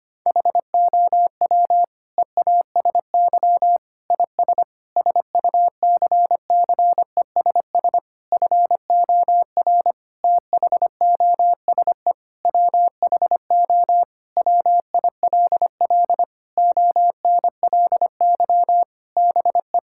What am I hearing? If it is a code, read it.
Morse code, heard as a keyed tone, about 25 wpm, 700 Hz. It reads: HOW EASY IS SUCCESS FOR THOSE WHO WILL ONLY BE
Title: Sentences from Top 1000 Words - Review Part 6 - 25wpm